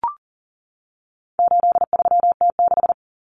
Morse code, heard as a keyed tone, about 40 wpm, 700 Hz. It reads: 83T6
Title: Common Cut Numbers - 4-Digit 40wpm